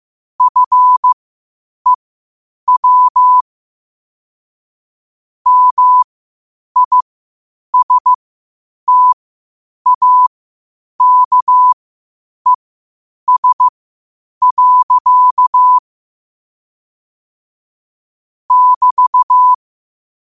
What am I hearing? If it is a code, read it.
Morse code, heard as a keyed tone, about 15 wpm, spaced as for 8 wpm, 1000 Hz. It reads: FEW MISTAKES. <BT>
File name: qotdfiles / 8wpm